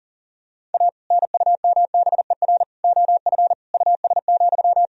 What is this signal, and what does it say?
Morse code, heard as a keyed tone, about 40 wpm, 700 Hz. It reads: A NUMBER OF US,
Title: Sentences from Top 1000 Words - Review Part 1 - 40wpm